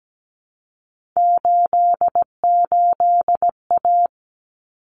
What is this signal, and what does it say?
Morse code, heard as a keyed tone, about 17 wpm, 700 Hz. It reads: 88A
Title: Common Cut Numbers - 3-Digit 17wpm